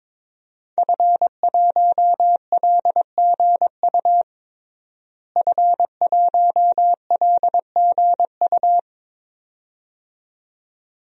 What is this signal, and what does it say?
Morse code, heard as a keyed tone, about 22 wpm, 700 Hz. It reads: F1LGU F1LGU
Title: DXCC-100 Call Signs - 5 Characters - Repeated 2x - 22wpm